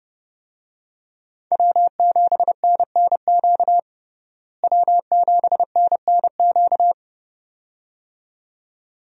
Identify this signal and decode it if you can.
Morse code, heard as a keyed tone, about 29 wpm, 700 Hz. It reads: W7NNQ W7NNQ
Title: USA Call Signs - 5 Characters - Repeated 2x - 30wpm